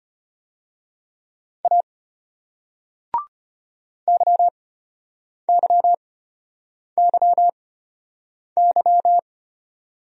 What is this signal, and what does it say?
Morse code, heard as a keyed tone, about 34 wpm, spaced as for 8 wpm, 700 Hz. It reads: A YYYY